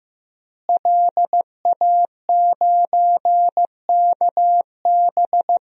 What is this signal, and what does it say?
Morse code, heard as a keyed tone, about 16 wpm, 700 Hz. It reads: LA9KB
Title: DXCC-100 Call Signs - 5 Characters - Repeated 3x - 15wpm